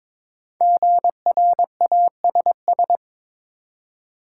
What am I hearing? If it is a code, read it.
Morse code, heard as a keyed tone, about 22 wpm, 700 Hz. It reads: GRASS
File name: 5 Letter Words - First 3 Letters Given - 22wpm